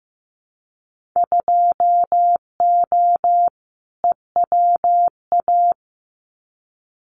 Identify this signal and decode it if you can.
Morse code, heard as a keyed tone, about 15 wpm, 700 Hz. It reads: 2O EWA
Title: Winter Field Day Exchanges Encore 15wpm